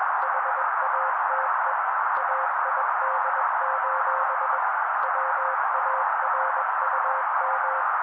Morse code, heard as a keyed tone, about 21 wpm, 530 Hz. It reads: HATE AID: WARUM